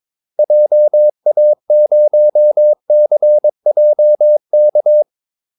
Morse code, heard as a keyed tone, about 22 wpm, 600 Hz. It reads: JA0CJK
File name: Callsigns-22wpm